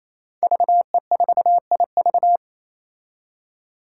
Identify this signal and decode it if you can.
Morse code, heard as a keyed tone, about 28 wpm, 700 Hz. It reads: VE4IV
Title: Five Challenging Letter-Number Combinations 28wpm